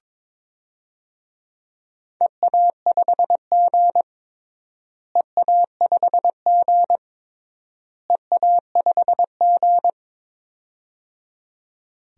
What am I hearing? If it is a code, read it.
Morse code, heard as a keyed tone, about 22 wpm, 700 Hz. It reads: EA5G EA5G EA5G
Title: DXCC-100 Call Signs - 4 Characters - Repeated 3x - 22wpm